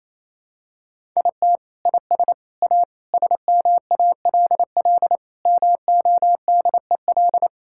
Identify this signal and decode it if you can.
Morse code, heard as a keyed tone, about 28 wpm, 700 Hz. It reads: IT IS A SMALL MODEL
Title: Sentences from Top 1000 Words - Review Part 5 - 28wpm